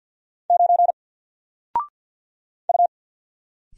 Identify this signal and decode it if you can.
Morse code, heard as a keyed tone, about 54 wpm, 700 Hz. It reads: 9 U
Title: Single Character (Letter:Number:Punctuation:Prosign) 50wpm